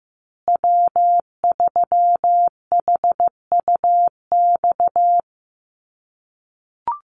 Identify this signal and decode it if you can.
Morse code, heard as a keyed tone, about 15 wpm, 700 Hz. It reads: W3HUX
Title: USA Call Signs Repeated 2x 15wpm